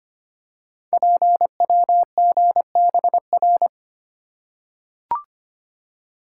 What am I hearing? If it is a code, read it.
Morse code, heard as a keyed tone, about 25 wpm, 700 Hz. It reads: PWGBR
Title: Code Groups - 5 Letters - 25wpm